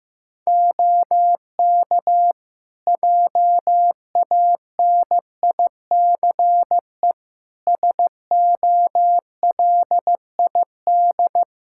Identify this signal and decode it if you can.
Morse code, heard as a keyed tone, about 16 wpm, 700 Hz. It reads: OK JANICE SOLID
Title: Practice QSO - Exchange 3 - More Info - 15wpm